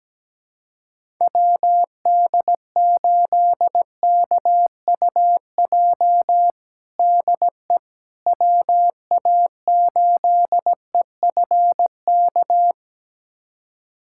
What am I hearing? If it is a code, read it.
Morse code, heard as a keyed tone, about 17 wpm, 700 Hz. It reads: WD8KUJ DE WA8EFK
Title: USA Callsign DE Callsign 17wpm